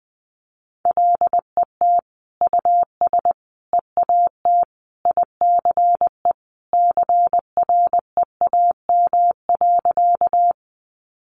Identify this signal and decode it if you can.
Morse code, heard as a keyed tone, about 20 wpm, 700 Hz. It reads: LET US EAT ICE CREAM.
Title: Sentences from Top 1500 Words 20wpm